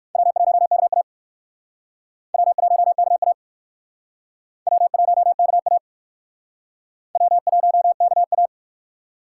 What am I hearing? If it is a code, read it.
Morse code, heard as a keyed tone, about 61 wpm, 700 Hz. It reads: W1KA W1KA W1KA W1KA